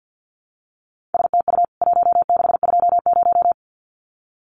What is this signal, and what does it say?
Morse code, heard as a keyed tone, about 50 wpm, 700 Hz. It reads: STU 1620
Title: CWT Exchange 50wpm